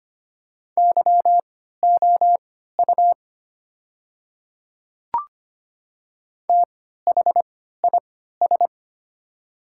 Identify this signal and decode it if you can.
Morse code, heard as a keyed tone, about 25 wpm, spaced as for 14 wpm, 700 Hz. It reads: YOU THIS
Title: 3x Character Spacing - Top 100 Words - 25wpm